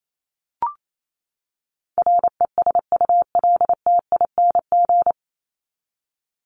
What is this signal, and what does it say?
Morse code, heard as a keyed tone, about 28 wpm, 700 Hz. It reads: RESULTING
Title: ING Suffix - Top 100 Words with Base Words 28wpm